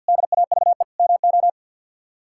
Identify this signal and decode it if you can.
Morse code, heard as a keyed tone, about 51 wpm, 700 Hz. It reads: DAVE MO
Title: K1USN SST QSO Practice - 50wpm